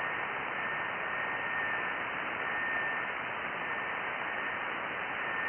FT8.
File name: IC-7100